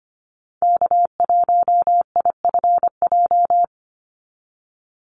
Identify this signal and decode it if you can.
Morse code, heard as a keyed tone, about 25 wpm, 700 Hz. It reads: K1IFJ
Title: Call Signs Repeated 3x - 25wpm